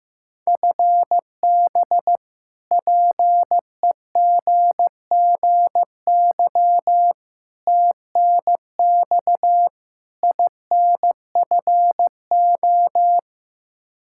Morse code, heard as a keyed tone, about 15 wpm, 700 Hz. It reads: FB PEGGY TNX INFO
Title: Practice QSO - Exchange 3 - More Info - 15wpm